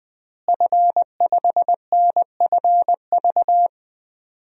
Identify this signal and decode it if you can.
Morse code, heard as a keyed tone, about 21 wpm, 700 Hz. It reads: F5NFV
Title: DXCC-100 Call Signs - 5 Characters - Repeated 3x - 20wpm